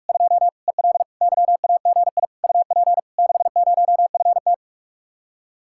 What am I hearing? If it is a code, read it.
Morse code, heard as a keyed tone, about 48 wpm, 700 Hz. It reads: 2 EL YAGI UP 60FT